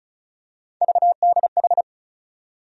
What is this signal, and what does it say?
Morse code, heard as a keyed tone, about 36 wpm, 700 Hz. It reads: VDH
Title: USA Call Sign Postfixes - 3 Letters 35wpm